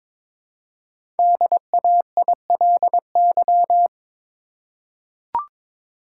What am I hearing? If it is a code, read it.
Morse code, heard as a keyed tone, about 22 wpm, 700 Hz. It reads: DAILY
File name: practice / 22wpm / 5 Letter Words - First 3 Letters Given - 22wpm